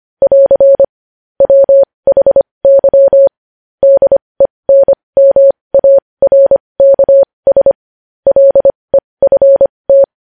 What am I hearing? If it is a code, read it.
Morse code, heard as a keyed tone, about 25 wpm, spaced as for 20 wpm, 550 Hz. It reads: <AR> WHY DENMARKS LEFT
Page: news-headlines-in-morse-code-at-20-wpm